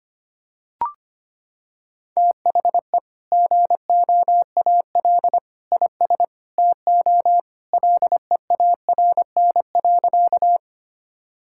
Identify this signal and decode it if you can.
Morse code, heard as a keyed tone, about 25 wpm, 700 Hz. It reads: THE GOAL IS TO LEARN.
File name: practice / 25wpm / Sentences with 2-Word Phrases - 25wpm